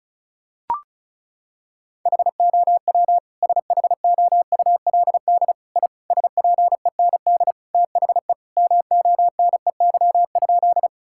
Morse code, heard as a keyed tone, about 36 wpm, 700 Hz. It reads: HOW SHOULD I SPEND THE MONEY?
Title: Sentences from Top 1000 Words - Review Part 5 - 35wpm